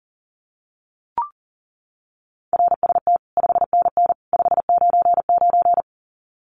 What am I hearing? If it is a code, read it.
Morse code, heard as a keyed tone, about 40 wpm, 700 Hz. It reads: RST 5NN 599
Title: Common QSO Phrases 40wpm